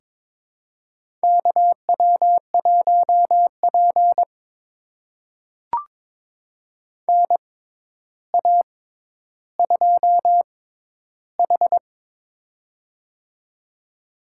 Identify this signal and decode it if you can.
Morse code, heard as a keyed tone, about 22 wpm, spaced as for 11 wpm, 700 Hz. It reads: KW1P NA2H
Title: USA Call Signs - 6x Character Spacing - 22wpm